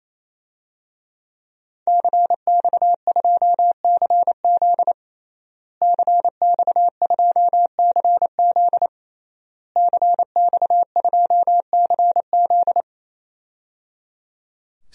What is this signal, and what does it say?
Morse code, heard as a keyed tone, about 28 wpm, 700 Hz. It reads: CX2CZ CX2CZ CX2CZ
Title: DXCC-100 Call Signs - 5 Characters - Repeated 3x - 28wpm